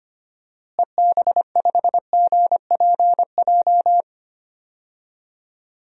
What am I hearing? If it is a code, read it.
Morse code, heard as a keyed tone, about 25 wpm, 700 Hz. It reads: EB5GPJ
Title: DXCC-100 Call Signs - 6 Characters - Repeated 2x - 25wpm